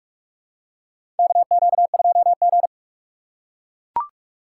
Morse code, heard as a keyed tone, about 44 wpm, 700 Hz. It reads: KQ2G
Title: USA Call Signs - 4 Characters - 6x Character Spacing - 45wpm